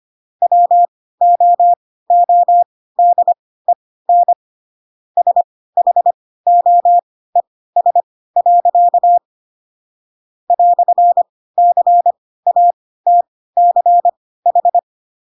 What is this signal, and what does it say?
Morse code, heard as a keyed tone, about 25 wpm, spaced as for 16 wpm, 700 Hz. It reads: WOODEN SHOES. "CATCH
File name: Pinocchio / Pinocchio ch3 10 25 ik1hge